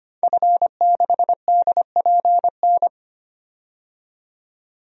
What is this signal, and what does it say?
Morse code, heard as a keyed tone, about 25 wpm, 700 Hz. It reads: F6DPN
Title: DXCC-100 Call Signs - 5 Characters - 25wpm